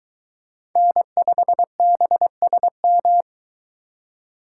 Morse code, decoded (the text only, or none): N5BSM